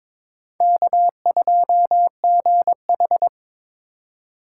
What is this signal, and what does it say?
Morse code, heard as a keyed tone, about 26 wpm, 700 Hz. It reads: K2GH
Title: Speed Racing - USA 1x2 Call Signs 22wpm